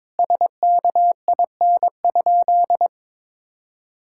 Morse code, heard as a keyed tone, about 22 wpm, 700 Hz. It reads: SKIN?
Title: Sentences from Top 1600 Words 22wpm